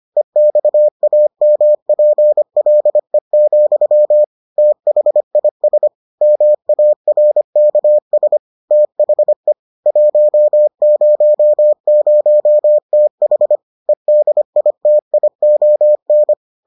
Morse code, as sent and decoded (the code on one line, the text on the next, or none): . -..- .- -- .--. .-.. --..-- / - .... .. ... / -- .- .-. -.- ... / - .... / .---- ----- ----- - .... / . -.. .. - .. --- -.
EXAMPLE, THIS MARKS THE 100TH EDITION